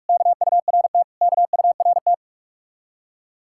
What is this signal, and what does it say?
Morse code, heard as a keyed tone, about 44 wpm, 700 Hz. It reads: KURT KURT